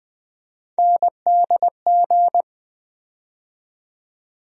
Morse code, heard as a keyed tone, about 20 wpm, 700 Hz. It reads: NDG
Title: USA Call Sign Postfixes - 3 Letters 20wpm